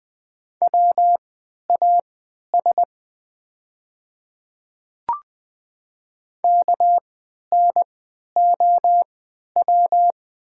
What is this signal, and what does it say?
Morse code, heard as a keyed tone, about 20 wpm, spaced as for 11 wpm, 700 Hz. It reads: WAS KNOW
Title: 3x Character Spacing - Top 100 Words - 20wpm